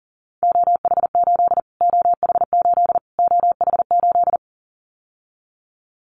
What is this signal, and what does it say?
Morse code, heard as a keyed tone, about 40 wpm, 700 Hz. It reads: OH8 OH8 OH8
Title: Callsign Prefixes for Top 25 DXCC - Repeated 3x - 40wpm